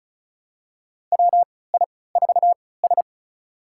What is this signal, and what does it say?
Morse code, heard as a keyed tone, about 36 wpm, spaced as for 20 wpm, 700 Hz. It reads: WI4S
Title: USA Call Signs - 4 Characters - 3x Character Spacing - 35wpm